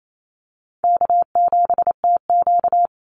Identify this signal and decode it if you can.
Morse code, heard as a keyed tone, about 28 wpm, 700 Hz. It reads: K7TQ
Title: NA SOTA Chaser Callsigns 28wpm